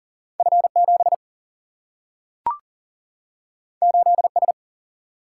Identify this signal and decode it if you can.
Morse code, heard as a keyed tone, about 40 wpm, 700 Hz. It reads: F 7 8 S
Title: Single-Letter Single-Number 40wpm